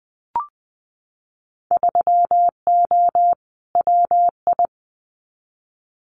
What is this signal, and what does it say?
Morse code, heard as a keyed tone, about 20 wpm, 700 Hz. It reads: 3O WI
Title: Winter Field Day Exchanges 20wpm